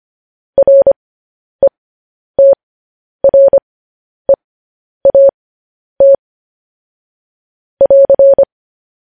Morse code, heard as a keyed tone, about 25 wpm, spaced as for 10 wpm, 550 Hz. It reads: RETREAT <AR>
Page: News Headlines in Morse Code at 10 WPM